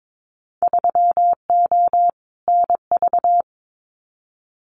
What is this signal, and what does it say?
Morse code, heard as a keyed tone, about 22 wpm, 700 Hz. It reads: 3O NV
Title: Winter Field Day Exchanges Encore 22wpm